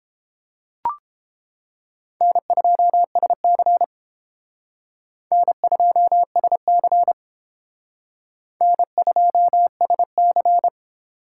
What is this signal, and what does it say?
Morse code, heard as a keyed tone, about 28 wpm, 700 Hz. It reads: N2SC N2SC N2SC